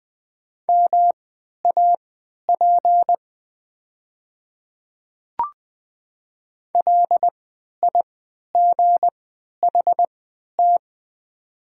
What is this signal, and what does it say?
Morse code, heard as a keyed tone, about 20 wpm, spaced as for 11 wpm, 690 Hz. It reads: MAP LIGHT
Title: 3x Character Spacing - Top 500 Words - 20wpm